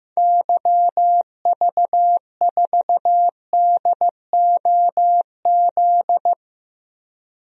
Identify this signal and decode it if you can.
Morse code, heard as a keyed tone, about 15 wpm, 700 Hz. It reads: YV4DOZ
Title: DXCC-100 Call Signs - 6 Characters - 15wpm